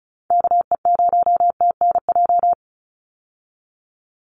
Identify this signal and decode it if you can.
Morse code, heard as a keyed tone, about 34 wpm, 700 Hz. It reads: KE0TNJ
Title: USA Call Signs 35wpm